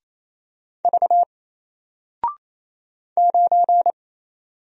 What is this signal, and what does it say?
Morse code, heard as a keyed tone, about 27 wpm, 700 Hz. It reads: V 9
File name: Single-Letter-Number-28wpm